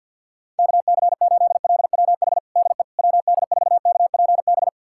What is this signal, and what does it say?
Morse code, heard as a keyed tone, about 53 wpm, 690 Hz. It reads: KC8LWH DE WD4KPB
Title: USA Callsign DE Callsign 50wpm